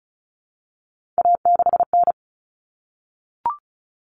Morse code, heard as a keyed tone, about 34 wpm, 700 Hz. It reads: A6N
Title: All Cut Numbers - 3-Digit 35wpm